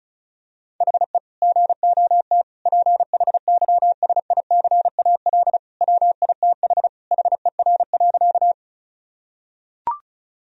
Morse code, heard as a keyed tone, about 36 wpm, 700 Hz. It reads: HE GOT PHYSICAL WITH HER.
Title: Sentences from Top 1000 Words - Review Part 4 - 35wpm